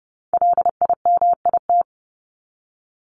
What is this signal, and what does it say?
Morse code, heard as a keyed tone, about 30 wpm, 700 Hz. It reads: LIMIT
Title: ED Suffix - Top 100 Words with Base Words 30wpm